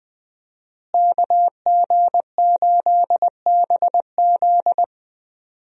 Morse code, heard as a keyed tone, about 20 wpm, 700 Hz. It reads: KG8BZ
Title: USA Call Signs - 5 Characters - Repeated 2x - 20wpm